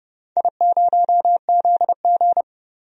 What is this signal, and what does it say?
Morse code, heard as a keyed tone, about 30 wpm, 700 Hz. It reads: I0ZG
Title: DXCC-100 Call Signs - 4 Characters - 30wpm